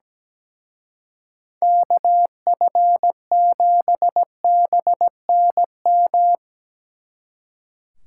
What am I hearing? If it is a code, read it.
Morse code, heard as a keyed tone, about 19 wpm, 700 Hz. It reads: KF7BNM
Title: Speed Racing - USA 2x3 Call Signs 17wpm